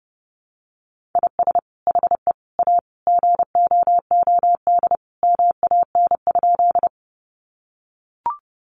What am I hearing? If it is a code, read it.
Morse code, heard as a keyed tone, about 30 wpm, 700 Hz. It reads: IS HE A GOOD MAN?
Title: Sentences from Top 200 Words 30wpm